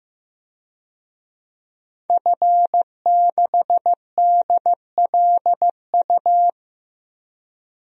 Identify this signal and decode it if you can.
Morse code, heard as a keyed tone, about 15 wpm, 700 Hz. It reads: F6DLU
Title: DXCC-100 Call Signs - 5 Characters - Repeated 2x - 15wpm